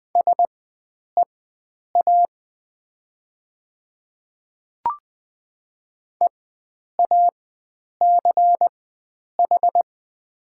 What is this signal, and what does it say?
Morse code, heard as a keyed tone, about 20 wpm, spaced as for 9 wpm, 700 Hz. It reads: SEA EACH
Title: 4x Character Spacing - Top 500 Words - 20wpm